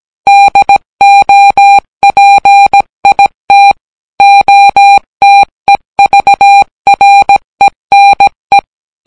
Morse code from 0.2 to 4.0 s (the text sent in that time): DOPIT